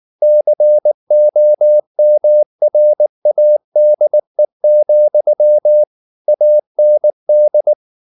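Morse code, heard as a keyed tone, about 19 wpm, 600 Hz. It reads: COMRADE, AND
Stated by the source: Morse code at 19 wpm from CW med Gnister, reading Ken Follett's novel Never in English.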